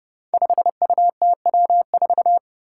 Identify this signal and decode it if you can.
Morse code, heard as a keyed tone, about 29 wpm, 700 Hz. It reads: 5UTW4
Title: Five Challenging Letter-Number Combinations 30wpm